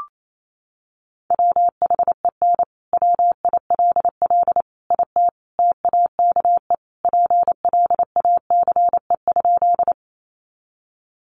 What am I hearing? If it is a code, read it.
Morse code, heard as a keyed tone, about 28 wpm, 700 Hz. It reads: WHEN WILL IT TAKE PLACE?
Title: Sentences from Top 200 Words 28wpm